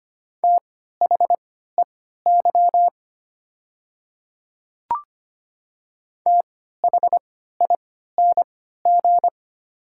Morse code, heard as a keyed tone, about 25 wpm, spaced as for 14 wpm, 700 Hz. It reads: THEY THING
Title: 3x Character Spacing - Top 100 Words - 25wpm